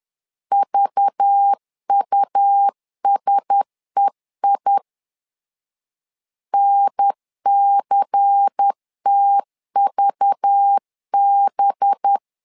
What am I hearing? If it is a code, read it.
Morse code, heard as a keyed tone, about 10 wpm, 800 Hz. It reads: VUSEI NCTVB
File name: LEZIONI / 5 LBVC / 5 LBVC 10wpm 2349